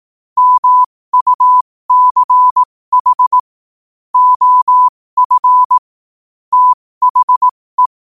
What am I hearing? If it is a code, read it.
Morse code, heard as a keyed tone, about 18 wpm, spaced as for 15 wpm, 1000 Hz. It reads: MUCH OF THE
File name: qotdfiles / 15wpm